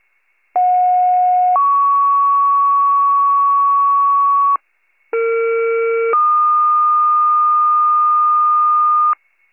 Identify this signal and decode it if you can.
Two-tone page, heard as a one-second tone, then a three-second tone, about 730 then 1120 Hz, about 460 then 1220 Hz. It is heard from 0.5 to 9.1 s.
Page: County Wide Paging Archive